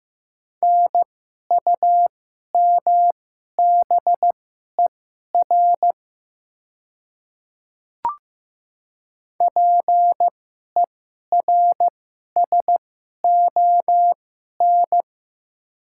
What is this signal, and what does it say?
Morse code, heard as a keyed tone, about 15 wpm, spaced as for 11 wpm, 700 Hz. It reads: NUMBER PERSON
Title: 2x Character Spacing - Top 500 Words - 15wpm